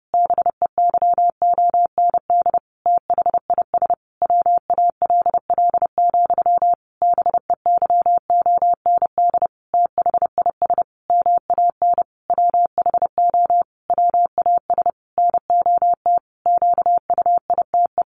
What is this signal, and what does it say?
Morse code, heard as a keyed tone, about 30 wpm, 700 Hz. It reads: BEYOND THIS WALL, BEYOND THIS MAN WHO WAS NOT QUITE
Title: A Sound Of Thunder 30wpm